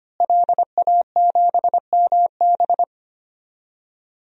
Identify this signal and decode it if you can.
Morse code, heard as a keyed tone, about 25 wpm, 700 Hz. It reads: LA7MB
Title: DXCC-100 Call Signs - 5 Characters - Repeated 2x - 25wpm